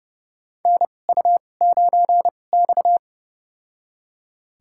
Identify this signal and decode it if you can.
Morse code, heard as a keyed tone, about 30 wpm, spaced as for 22 wpm, 700 Hz. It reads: NU9X
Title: USA Call Signs - 4 Characters - 2x Character Spacing - 30wpm